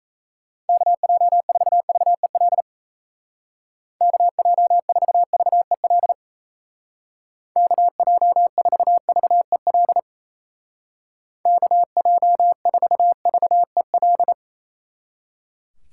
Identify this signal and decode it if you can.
Morse code, heard as a keyed tone, about 36 wpm, 700 Hz. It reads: KJ4VEL KJ4VEL KJ4VEL KJ4VEL